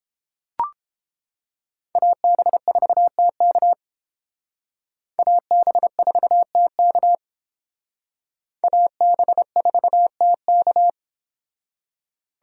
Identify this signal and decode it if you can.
Morse code, heard as a keyed tone, about 29 wpm, 700 Hz. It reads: AB4TK AB4TK AB4TK